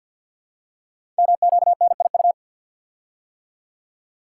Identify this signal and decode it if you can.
Morse code, heard as a keyed tone, about 48 wpm, 690 Hz. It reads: MQNIU
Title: Code Groups - 5 Letters - 50wpm